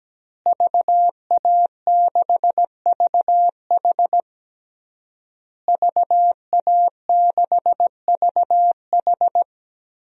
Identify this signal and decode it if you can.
Morse code, heard as a keyed tone, about 17 wpm, 700 Hz. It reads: VA6VH VA6VH
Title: DXCC-100 Call Signs - 5 Characters - Repeated 3x - 17wpm